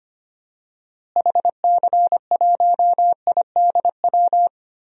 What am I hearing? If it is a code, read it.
Morse code, heard as a keyed tone, about 25 wpm, 700 Hz. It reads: HC1IDW
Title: DXCC-100 Call Signs - 6 Characters - Repeated 3x - 25wpm